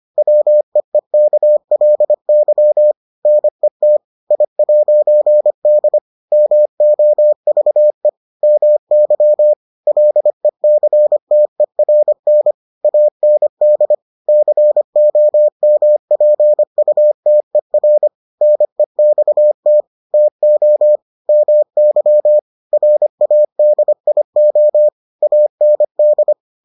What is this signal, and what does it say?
Morse code, heard as a keyed tone, about 25 wpm, 600 Hz. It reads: WEEKLY NET I'D MOVE MY LECTERN AND COMPUTER NEXT TO MY RADIO AND